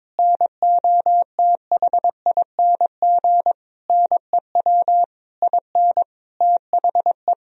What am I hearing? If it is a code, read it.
Morse code, heard as a keyed tone, about 22 wpm, 700 Hz. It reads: NOTHING NEW IN THE
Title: Sentences from Top 1000 Words - Review Part 3 - 22wpm